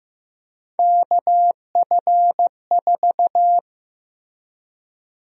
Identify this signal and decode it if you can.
Morse code, heard as a keyed tone, about 15 wpm, 700 Hz. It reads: KF4
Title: USA Call Sign Prefixes - Repeated 2x - 15wpm